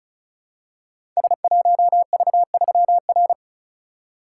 Morse code, heard as a keyed tone, about 36 wpm, 700 Hz. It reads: S1V3R
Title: Code Groups - 5 Letter:Numbers - 35wpm